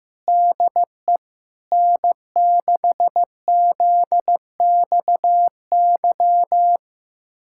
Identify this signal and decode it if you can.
Morse code, heard as a keyed tone, about 16 wpm, 700 Hz. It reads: DE N6ZXY
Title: USA Callsign DE Callsign 15wpm